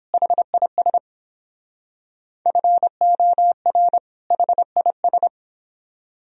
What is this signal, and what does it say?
Morse code, heard as a keyed tone, about 29 wpm, 700 Hz. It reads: HIS FOR HIS